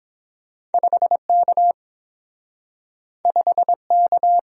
Morse code, heard as a keyed tone, about 30 wpm, 700 Hz. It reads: 5 K 5 K